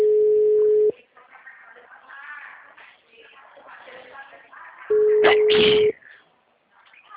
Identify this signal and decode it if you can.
Ringback tone (Europe, 425 Hz).
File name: out- -2005- - - .14793